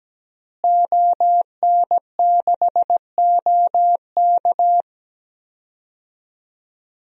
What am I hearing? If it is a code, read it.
Morse code, heard as a keyed tone, about 17 wpm, 700 Hz. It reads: ON6OK
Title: DXCC-100 Call Signs - 5 Characters - Repeated 2x - 17wpm